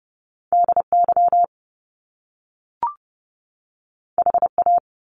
Morse code, heard as a keyed tone, about 30 wpm, 700 Hz. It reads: DY HA
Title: Top Two Letter Combinations in Words 30wpm